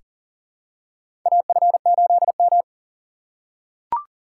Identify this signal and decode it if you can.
Morse code, heard as a keyed tone, about 38 wpm, 700 Hz. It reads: AF8M
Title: USA Call Signs - 6x Character Spacing - 40wpm